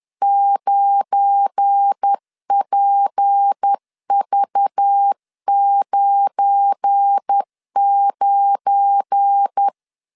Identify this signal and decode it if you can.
Morse code, heard as a keyed tone, about 10 wpm, 800 Hz. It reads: 9PV99
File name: LEZIONI / 7 GF9Z / 7 GF9 10wpm 2747